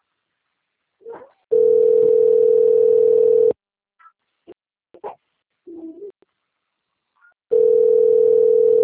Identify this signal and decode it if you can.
Ringback tone (North America), 440 + 480 Hz.